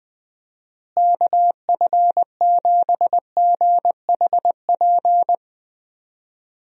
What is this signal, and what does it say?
Morse code, heard as a keyed tone, about 20 wpm, 700 Hz. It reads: KF7GHP
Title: USA Call Signs - 6 Characters - Repeated 2x - 20wpm